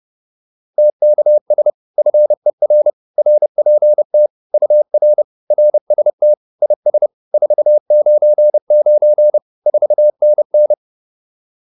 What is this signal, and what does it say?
Morse code, heard as a keyed tone, about 30 wpm, 600 Hz. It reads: TKS FER RPT UR RST IS 499 4NN